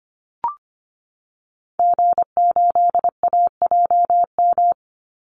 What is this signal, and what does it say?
Morse code, heard as a keyed tone, about 25 wpm, 700 Hz. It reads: G8AJM
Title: Call Signs Repeated 2x - 25wpm